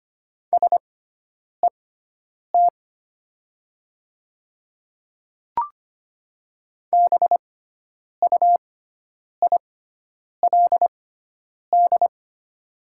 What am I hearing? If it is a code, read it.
Morse code, heard as a keyed tone, about 25 wpm, spaced as for 9 wpm, 700 Hz. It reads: SET BUILD
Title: 6x Character Spacing - Top 200 Words - 25wpm